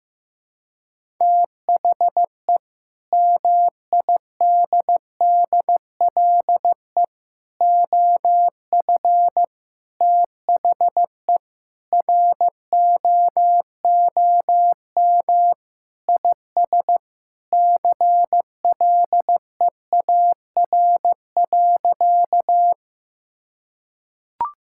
Morse code, heard as a keyed tone, about 15 wpm, 700 Hz. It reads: THE MIDDLE OF THE ROOM IS CLEAR.
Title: Sentences from Top 1000 Words - Review Part 6 - 15wpm